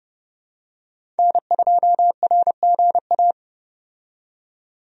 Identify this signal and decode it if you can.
Morse code, heard as a keyed tone, about 30 wpm, 700 Hz. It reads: N2RGA
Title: USA Call Signs - 5 Characters - 30wpm